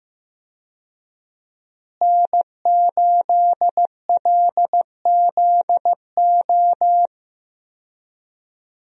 Morse code, heard as a keyed tone, about 15 wpm, 700 Hz. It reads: N8LZO
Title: USA Call Signs - 5 Characters - Repeated 3x - 15wpm